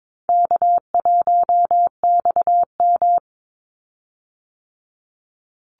Morse code, heard as a keyed tone, about 22 wpm, 690 Hz. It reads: K1XM
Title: Easy Call Signs Repeated 2x - 22wpm